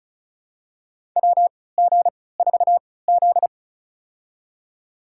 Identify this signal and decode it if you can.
Morse code, heard as a keyed tone, about 36 wpm, spaced as for 20 wpm, 700 Hz. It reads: WG4Z
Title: USA Call Signs - 3x Character Spacing - 35wpm